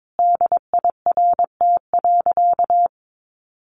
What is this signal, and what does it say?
Morse code, heard as a keyed tone, about 22 wpm, 700 Hz. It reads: DIRT.
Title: A Sound Of Thunder 22wpm